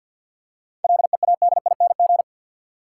Morse code, heard as a keyed tone, about 53 wpm, 700 Hz. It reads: LEADING
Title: Seven Letter Words - 50wpm